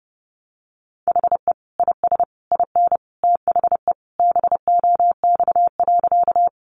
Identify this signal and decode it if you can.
Morse code, heard as a keyed tone, about 30 wpm, 700 Hz. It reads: HE IS IN THE BOX.
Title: Sentences from Top 500 Words 30wpm